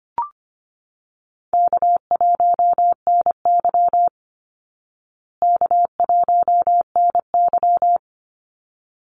Morse code, heard as a keyed tone, about 25 wpm, 700 Hz. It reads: K1NY K1NY
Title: Easy Call Signs Repeated 2x - 25wpm